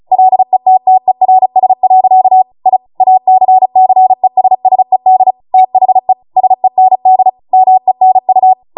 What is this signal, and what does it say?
Morse code, heard as a keyed tone, about 36 wpm, 750 Hz. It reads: LETTERS. I ACCESSED THE SEND MENU